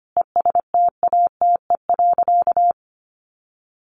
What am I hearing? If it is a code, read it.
Morse code, heard as a keyed tone, about 25 wpm, 700 Hz. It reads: ESTATE.
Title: Sentences from Top 1200 Words 25wpm